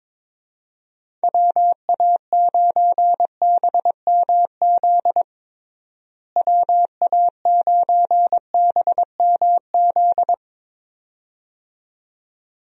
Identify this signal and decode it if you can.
Morse code, heard as a keyed tone, about 22 wpm, 700 Hz. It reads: WA9BMZ WA9BMZ
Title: USA Call Signs - 6 Characters - Repeated 2x - 22wpm